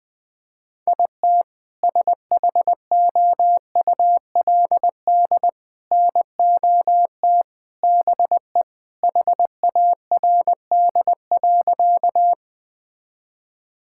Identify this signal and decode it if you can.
Morse code, heard as a keyed tone, about 20 wpm, 700 Hz. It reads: IT SHOULD NOT BE HARD.
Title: Sentences with 2-Word Phrases - 20wpm